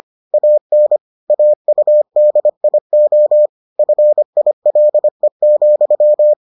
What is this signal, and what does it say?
Morse code, heard as a keyed tone, about 25 wpm, 600 Hz. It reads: AN AUDIO FILE,